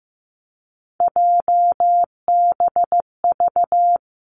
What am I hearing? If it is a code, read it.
Morse code, heard as a keyed tone, about 15 wpm, 700 Hz. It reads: JBV
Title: Three Challenging Letter-Number Combinations 15wpm